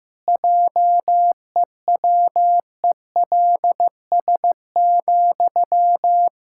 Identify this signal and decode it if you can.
Morse code, heard as a keyed tone, about 15 wpm, 700 Hz. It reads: JEWELS,
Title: Pangram Challenge v2 - 15wpm